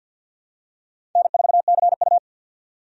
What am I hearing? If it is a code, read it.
Morse code, heard as a keyed tone, about 51 wpm, 700 Hz. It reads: N4CU
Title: USA Call Signs - 4 Characters - 50wpm